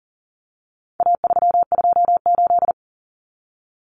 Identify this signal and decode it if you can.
Morse code, heard as a keyed tone, about 40 wpm, 700 Hz. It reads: A328
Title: Common Cut Numbers - 4-Digit 40wpm